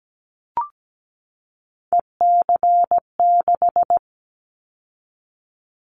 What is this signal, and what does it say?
Morse code, heard as a keyed tone, about 17 wpm, 700 Hz. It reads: EC6
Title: Callsign Prefixes for Top 100 DXCC - 17wpm